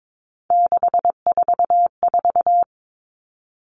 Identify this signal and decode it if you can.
Morse code, heard as a keyed tone, about 22 wpm, 700 Hz. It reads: 644
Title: Numbers - 3-Digit 22wpm